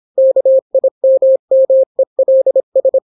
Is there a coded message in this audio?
Yes: Morse code from 0.2 to 3.0 s, heard as a keyed tone, about 26 wpm, 530 Hz. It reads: KIMMELS